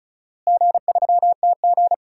Morse code, heard as a keyed tone, about 36 wpm, 700 Hz. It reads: G3TG
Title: DXCC-100 Call Signs - 4 Characters - Repeated 2x - 35wpm